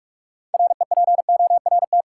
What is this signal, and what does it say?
Morse code, heard as a keyed tone, about 45 wpm, 700 Hz. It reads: REPORT
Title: Sentences from Top 900 Words - New Word Drill - 45wpm